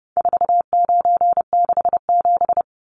Morse code, heard as a keyed tone, about 30 wpm, 700 Hz. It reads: 4967
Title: Numbers - 4-Digit 30wpm